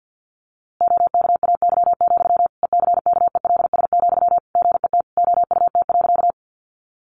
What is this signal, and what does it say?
Morse code, heard as a keyed tone, about 50 wpm, 700 Hz. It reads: OKAY, ECKELS, GET OUT.
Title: A Sound Of Thunder 50wpm